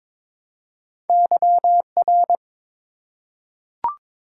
Morse code, heard as a keyed tone, about 22 wpm, 700 Hz. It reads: YR
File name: practice / 22wpm / Two Letter-Number Combinations - 22wpm